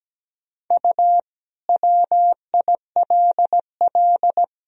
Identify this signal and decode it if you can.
Morse code, heard as a keyed tone, about 17 wpm, 700 Hz. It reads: U WILL
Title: Sentences from Top 300 Words and Abbreviations 17wpm